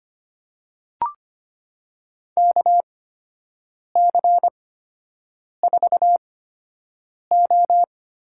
Morse code, heard as a keyed tone, about 25 wpm, spaced as for 7 wpm, 700 Hz. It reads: KC4O
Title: USA Call Signs - 6 Characters - 8x Character Spacing - 25wpm